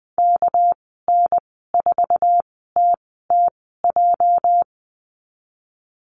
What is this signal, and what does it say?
Morse code, heard as a keyed tone, about 20 wpm, spaced as for 14 wpm, 700 Hz. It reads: KN4TTJ
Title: 245 Lesson 39g - Letter z - Call Signs 20wpm